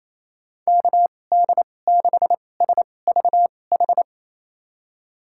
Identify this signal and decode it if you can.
Morse code, heard as a keyed tone, about 28 wpm, spaced as for 20 wpm, 700 Hz. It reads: KD6SVH
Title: USA Call Signs - 2x Character Spacing - 28wpm